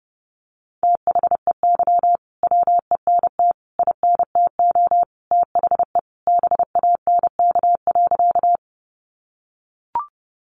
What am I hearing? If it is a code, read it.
Morse code, heard as a keyed tone, about 30 wpm, 700 Hz. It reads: THEY WENT INTO THE BANK.
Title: Sentences from Top 900 Words 30wpm